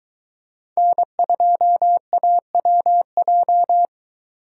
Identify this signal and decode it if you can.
Morse code, heard as a keyed tone, about 23 wpm, 700 Hz. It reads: N2AWJ